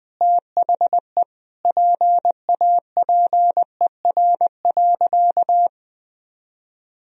Morse code, heard as a keyed tone, about 20 wpm, 700 Hz. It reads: THE PAPER.
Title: Sentences from Top 300 Words and Abbreviations 20wpm